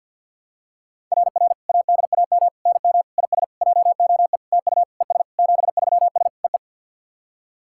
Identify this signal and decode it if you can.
Morse code, heard as a keyed tone, about 52 wpm, 690 Hz. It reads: RR ADAM NM IS JOE TU ES 73S EE
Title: SOTA QSO Practice 50wpm